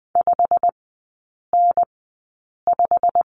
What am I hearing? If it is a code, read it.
Morse code, heard as a keyed tone, about 20 wpm, spaced as for 8 wpm, 690 Hz. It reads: 5N5
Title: 093 Lesson 16e - Letter c - Three Characters 20wpm